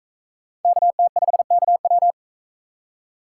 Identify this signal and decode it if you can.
Morse code, heard as a keyed tone, about 42 wpm, 700 Hz. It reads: KT5KW